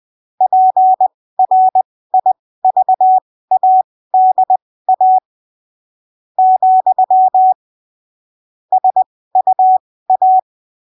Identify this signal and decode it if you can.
Morse code, heard as a keyed tone, about 20 wpm, spaced as for 15 wpm, 750 Hz. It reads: PRIVADA , SUA
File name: arquivos / texto / texto 28 20ppm